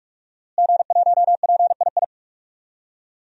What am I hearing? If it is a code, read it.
Morse code, heard as a keyed tone, about 44 wpm, 700 Hz. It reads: G1PII
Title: DXCC-100 Call Signs - 5 Characters - Repeated 3x - 45wpm